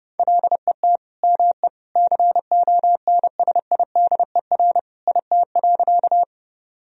Morse code, heard as a keyed tone, about 30 wpm, 700 Hz. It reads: LET ME CONSIDER IT.
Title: Sentences from Top 1000 Words - Review Part 4 - 30wpm